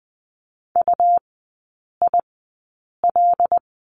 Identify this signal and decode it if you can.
Morse code, heard as a keyed tone, about 20 wpm, spaced as for 8 wpm, 700 Hz. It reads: UIL
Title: 100 Lesson 17e - Letter u - Three Characters 20wpm